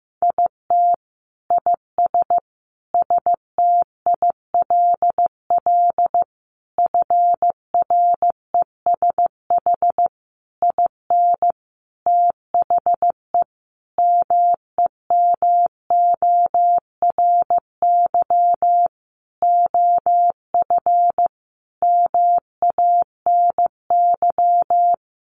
Morse code, heard as a keyed tone, about 15 wpm, 700 Hz. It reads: IT IS STILL FRESH IN THE MEMORY OF MANY
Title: Sentences from Top 1300 Words 15wpm